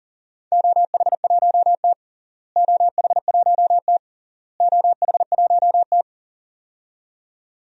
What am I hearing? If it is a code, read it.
Morse code, heard as a keyed tone, about 40 wpm, 700 Hz. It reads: OH1T OH1T OH1T
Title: DXCC-100 Call Signs - 4 Characters - Repeated 3x - 40wpm